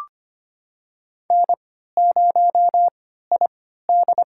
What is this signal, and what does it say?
Morse code, heard as a keyed tone, about 25 wpm, spaced as for 14 wpm, 700 Hz. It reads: N0ID
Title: USA Call Signs - 4 Characters - 3x Character Spacing - 25wpm